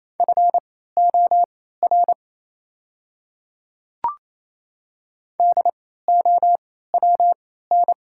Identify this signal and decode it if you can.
Morse code, heard as a keyed tone, about 27 wpm, spaced as for 16 wpm, 700 Hz. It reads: FOR DOWN
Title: 3x Character Spacing - Top 300 Words - 28wpm